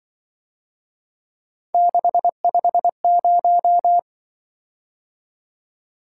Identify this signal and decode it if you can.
Morse code, heard as a keyed tone, about 24 wpm, 700 Hz. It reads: 650
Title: Speed-Racing - Three Letter-Number Combinations - 20wpm